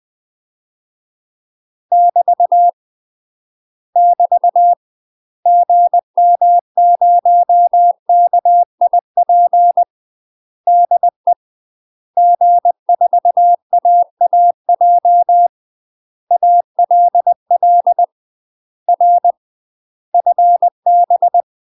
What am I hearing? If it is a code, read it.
Morse code, heard as a keyed tone, about 20 wpm, 700 Hz. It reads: <BT> <BT> GM0KIP DE G4AAJ ALL R FB
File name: UK MASTER QSOs TRAINING / 20WPM / UK MASTER QSOs 0014